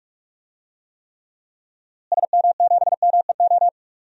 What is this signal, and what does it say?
Morse code, heard as a keyed tone, about 46 wpm, 700 Hz. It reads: SM7MEO
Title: DXCC-100 Call Signs - 6 Characters - Repeated 3x - 45wpm